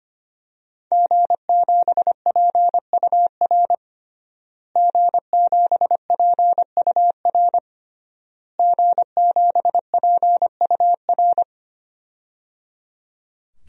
Morse code, heard as a keyed tone, about 25 wpm, 700 Hz. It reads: G7PUR G7PUR G7PUR
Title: DXCC-100 Call Signs - 5 Characters - Repeated 3x - 25wpm